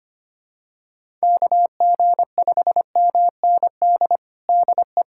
Morse code, heard as a keyed tone, about 25 wpm, 700 Hz. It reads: KG5MND DE
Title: USA Callsign DE Callsign 25wpm